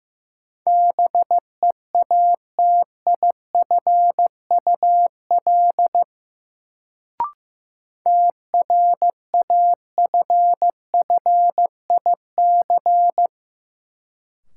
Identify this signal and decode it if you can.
Morse code, heard as a keyed tone, about 15 wpm, 700 Hz. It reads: BEATIFUL TRAFFIC
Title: Sentences from Top 1300 Words - New Word Drill Alt - 15wpm